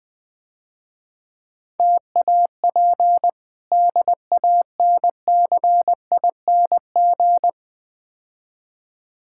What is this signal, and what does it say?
Morse code, heard as a keyed tone, about 20 wpm, 700 Hz. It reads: TAP DANCING
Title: Difficult Words - Speed-Racing with Hints - 20wpm